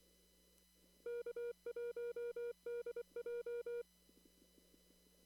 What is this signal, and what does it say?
Morse code, heard as a keyed tone, about 24 wpm, 480 Hz. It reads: K1DJ
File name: CW / kt5j wpx cw-op-k5pi